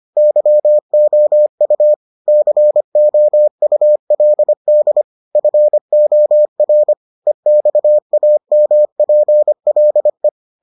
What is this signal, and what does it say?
Morse code, heard as a keyed tone, about 25 wpm, 600 Hz. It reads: YOU COULD FOR EXAMPLE